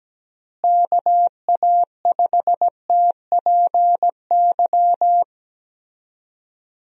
Morse code, heard as a keyed tone, about 17 wpm, 700 Hz. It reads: KA5TPY
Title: USA Call Signs - 6 Characters - Repeated 3x - 17wpm